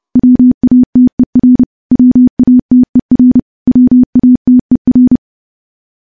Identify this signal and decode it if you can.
Morse code, heard as a keyed tone, about 28 wpm, 250 Hz. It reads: WATER WATER WATER
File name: morse / morse-training-30WPM